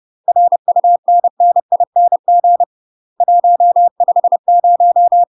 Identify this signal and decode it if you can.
Morse code, heard as a keyed tone, about 30 wpm, 700 Hz. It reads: RUNNING 150
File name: UK MASTER QSOs TRAINING / 30WPM / UK MASTER QSOs 0024